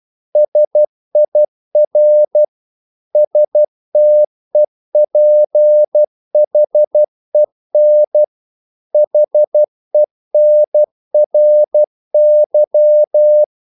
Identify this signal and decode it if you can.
Morse code, heard as a keyed tone, about 12 wpm, 600 Hz. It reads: SIR STEPHEN HENRY
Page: English quotations in CW (telegraphy - morse)